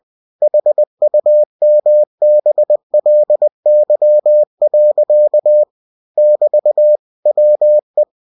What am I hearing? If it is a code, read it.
Morse code, heard as a keyed tone, about 20 wpm, 600 Hz. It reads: HUMBLY. <BT> WE
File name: cw-books / winnie-the-pooh / winnie-20wpm-page-0061